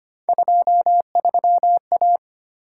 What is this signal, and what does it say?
Morse code, heard as a keyed tone, about 25 wpm, 700 Hz. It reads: 23A
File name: Three Letter-Number Combinations - 25wpm